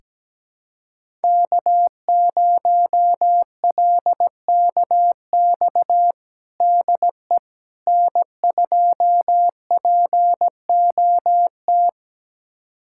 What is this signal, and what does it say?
Morse code, heard as a keyed tone, about 17 wpm, 700 Hz. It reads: K0LKX DE N2POT
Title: USA Callsign DE Callsign - 5 Characters - 17wpm